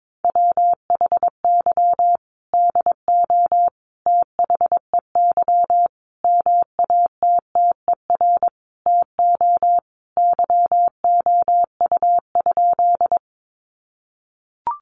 Morse code, heard as a keyed tone, about 22 wpm, 700 Hz. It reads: WHY DO THEY MATTER TO YOU?
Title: Sentences from Top 700 Words 22wpm